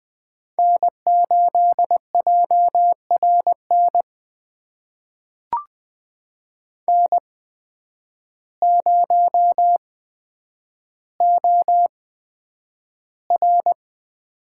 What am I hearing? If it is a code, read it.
Morse code, heard as a keyed tone, about 20 wpm, spaced as for 9 wpm, 700 Hz. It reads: N8JRN N0OR
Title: USA Call Signs - 8x Character Spacing - 20wpm